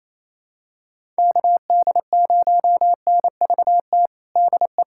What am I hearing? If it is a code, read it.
Morse code, heard as a keyed tone, about 28 wpm, 700 Hz. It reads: KD0NVT DE
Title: USA Callsign DE Callsign 28wpm